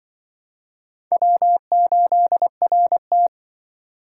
Morse code, heard as a keyed tone, about 25 wpm, 700 Hz. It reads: W8RT